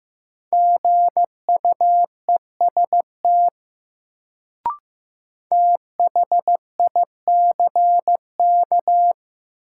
Morse code, heard as a keyed tone, about 15 wpm, 700 Hz. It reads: GUEST THICK
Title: Sentences from Top 1600 Words - New Word Drill - 15wpm